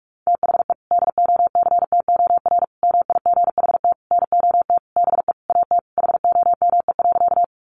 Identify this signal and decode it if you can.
Morse code, heard as a keyed tone, about 44 wpm, 700 Hz. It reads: THE DOCTOR MIGHT NOT BE AT HOME.
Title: Sentences from Top 1500 Words 45wpm